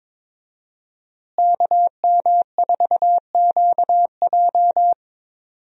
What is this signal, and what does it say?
Morse code, heard as a keyed tone, about 22 wpm, 700 Hz. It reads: KM4QJ
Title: USA Call Signs - 5 Characters - Repeated 2x - 22wpm